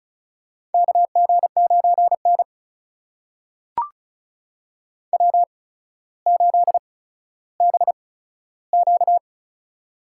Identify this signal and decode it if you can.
Morse code, heard as a keyed tone, about 36 wpm, spaced as for 15 wpm, 700 Hz. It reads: KG9N W8BQ
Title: USA Call Signs - 4 Characters - 8x Character Spacing - 35wpm